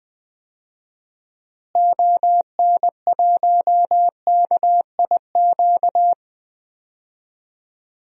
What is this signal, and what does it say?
Morse code, heard as a keyed tone, about 20 wpm, 700 Hz. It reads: ON1KIQ
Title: DXCC-100 Call Signs - 6 Characters - Repeated 2x - 20wpm